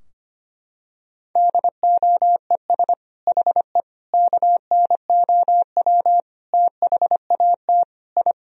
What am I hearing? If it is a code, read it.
Morse code, heard as a keyed tone, about 25 wpm, 700 Hz. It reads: DOES HE KNOW THAT I
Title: Sentences from Top 1000 Words - Review Part 4 - 25wpm